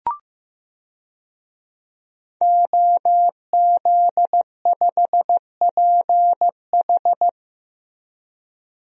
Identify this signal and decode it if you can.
Morse code, heard as a keyed tone, about 15 wpm, 700 Hz. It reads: OZ5PH
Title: DXCC-100 Call Signs - 5 Characters - Repeated 3x - 15wpm